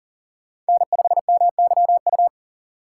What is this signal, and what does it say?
Morse code, heard as a keyed tone, about 40 wpm, 700 Hz. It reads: N5MYU
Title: USA Call Signs - 5 Characters - 3x Character Spacing - 40wpm